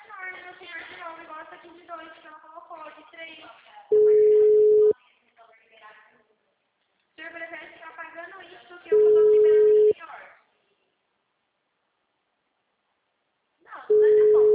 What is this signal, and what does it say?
Ringback tone (Europe, 425 Hz), with speech.